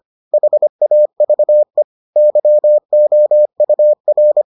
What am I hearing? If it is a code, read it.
Morse code, heard as a keyed tone, about 25 wpm, 600 Hz. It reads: HAVE YOUR